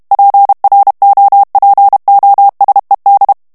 Morse code, heard as a keyed tone, about 32 wpm, 800 Hz. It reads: PROPOSED
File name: morse mp3 / ANGLAIS texte clair 3 vitesse 30 mots minute